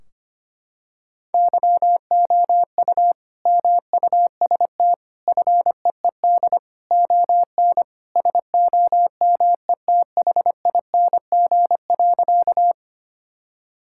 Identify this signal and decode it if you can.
Morse code, heard as a keyed tone, about 25 wpm, 700 Hz. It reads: YOU MUST FEED ON SOMETHING.
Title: Sentences from Top 1600 Words 25wpm